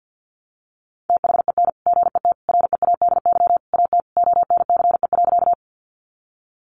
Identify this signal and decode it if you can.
Morse code, heard as a keyed tone, about 50 wpm, 700 Hz. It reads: THEN GET READY AT ONCE.
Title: Sentences from Top 400 Words 50wpm